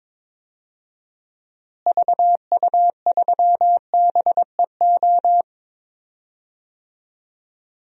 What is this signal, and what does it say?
Morse code, heard as a keyed tone, about 22 wpm, 700 Hz. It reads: VU3BEO
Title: DXCC-100 Call Signs - 6 Characters - Repeated 2x - 22wpm